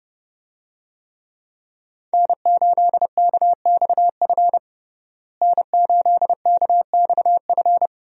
Morse code, heard as a keyed tone, about 30 wpm, 700 Hz. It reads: N8KXF N8KXF
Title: USA Call Signs - 5 Characters - Repeated 3x - 30wpm